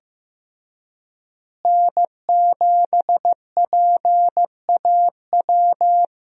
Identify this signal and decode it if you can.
Morse code, heard as a keyed tone, about 15 wpm, 700 Hz. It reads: N7PAW
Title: USA Call Signs - 5 Characters - Repeated 3x - 15wpm